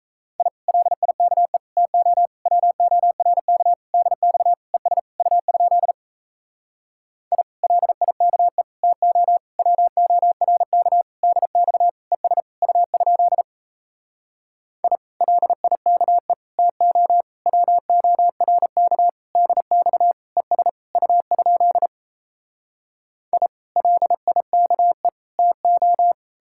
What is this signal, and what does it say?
Morse code, heard as a keyed tone, about 38 wpm, 700 Hz. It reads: I LIKE TO WORK DX ES U? I LIKE TO WORK DX ES U? I LIKE TO WORK DX ES U? I LIKE TO